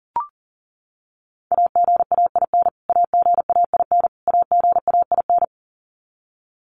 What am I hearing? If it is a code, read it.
Morse code, heard as a keyed tone, about 40 wpm, 700 Hz. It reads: AGAIN AGAIN AGAIN
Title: Top 500 Words Repeated 40wpm